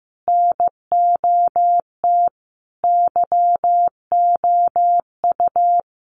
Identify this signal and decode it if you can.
Morse code, heard as a keyed tone, about 15 wpm, 700 Hz. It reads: NOT YOU
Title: Sets of 4 Words 15wpm